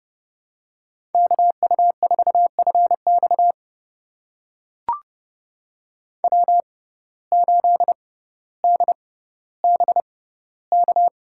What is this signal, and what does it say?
Morse code, heard as a keyed tone, about 30 wpm, spaced as for 15 wpm, 700 Hz. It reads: KU4FX W8DBK
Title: USA Call Signs - 5 Characters - 6x Character Spacing - 30wpm